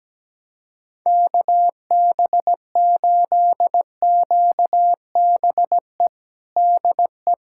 Morse code, heard as a keyed tone, about 17 wpm, 700 Hz. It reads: KB8QBE DE
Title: USA Callsign DE Callsign 17wpm